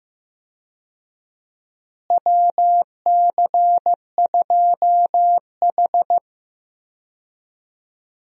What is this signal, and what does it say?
Morse code, heard as a keyed tone, about 15 wpm, 700 Hz. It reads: WC2H
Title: USA Call Signs - 4 Characters - Repeated 2x - 15wpm